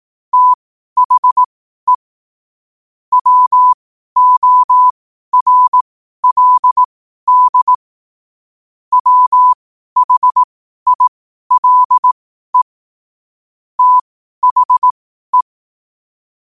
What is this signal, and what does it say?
Morse code, heard as a keyed tone, about 18 wpm, spaced as for 12 wpm, 1000 Hz. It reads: THE WORLD WHILE THE
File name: qotdfiles / 12wpm